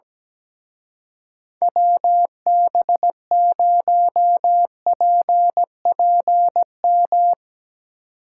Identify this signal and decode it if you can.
Morse code, heard as a keyed tone, about 17 wpm, 700 Hz. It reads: WB0PPM
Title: Speed Racing - USA Call Signs 17wpm